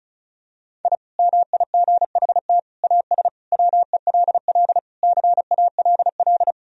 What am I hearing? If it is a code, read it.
Morse code, heard as a keyed tone, about 34 wpm, 700 Hz. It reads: I MIGHT AS WELL CALL
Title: Sentences from Top 1000 Words - Review Part 2 - 35wpm